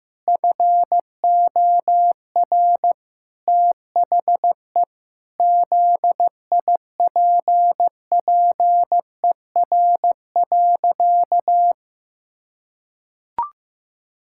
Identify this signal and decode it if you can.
Morse code, heard as a keyed tone, about 15 wpm, 700 Hz. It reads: FOR THE ZIPPER.
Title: Pangram Challenge v2 - 15wpm